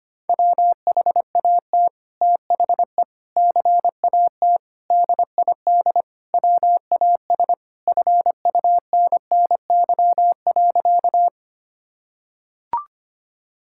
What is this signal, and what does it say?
Morse code, heard as a keyed tone, about 25 wpm, 700 Hz. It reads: WHAT THE CAT DID WAS FUNNY.
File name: practice / 25wpm / Sentences with 2-Word Phrases - 25wpm